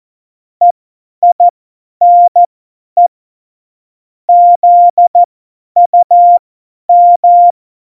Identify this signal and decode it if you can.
Morse code, heard as a keyed tone, about 14 wpm, spaced as for 10 wpm, 700 Hz. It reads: EINE ZUM
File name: Morsekurs / Witze / Witz 07 April 2020, 50 F70